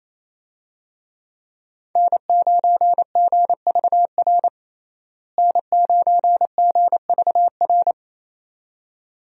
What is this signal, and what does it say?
Morse code, heard as a keyed tone, about 28 wpm, 700 Hz. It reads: N9GVR N9GVR
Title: USA Call Signs - 5 Characters - Repeated 2x - 28wpm